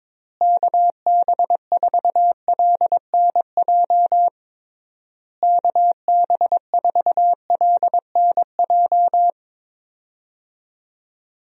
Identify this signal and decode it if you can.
Morse code, heard as a keyed tone, about 22 wpm, 700 Hz. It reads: KB4LNJ KB4LNJ
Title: USA Call Signs - 6 Characters - Repeated 3x - 22wpm